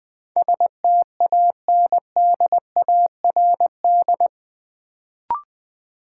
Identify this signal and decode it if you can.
Morse code, heard as a keyed tone, about 21 wpm, 700 Hz. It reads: STANDARD
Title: Sentences from Top 1000 Words - Review Part 6 - New Word Drill - 20wpm